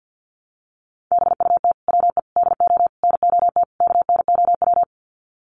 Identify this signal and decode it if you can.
Morse code, heard as a keyed tone, about 48 wpm, 700 Hz. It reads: BUT WE DO NOT KNOW
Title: Sets of 5 Words Encore2 50wpm